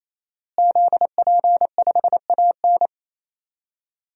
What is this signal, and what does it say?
Morse code, heard as a keyed tone, about 28 wpm, 700 Hz. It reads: ZP5AN
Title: DXCC-100 Call Signs - 5 Characters - Repeated 3x - 28wpm